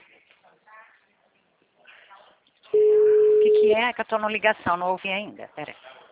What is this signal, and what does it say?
Ringback tone (Europe, 425 Hz), with speech.
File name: out- -2008- - - .30